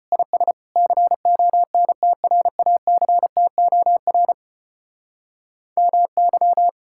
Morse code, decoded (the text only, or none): IS CONTRACTOR MY